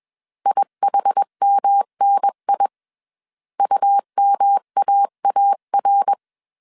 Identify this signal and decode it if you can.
Morse code, heard as a keyed tone, about 21 wpm, 800 Hz. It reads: IHMNI UMAAR